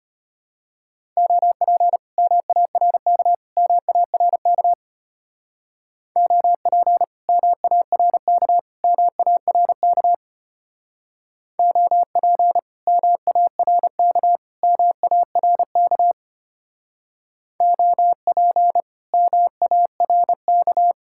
Morse code, heard as a keyed tone, about 30 wpm, 700 Hz. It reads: OP MARK MARK OP MARK MARK OP MARK MARK OP MARK
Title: Speed-Racing Common QSO Phrases Encore - 25wpm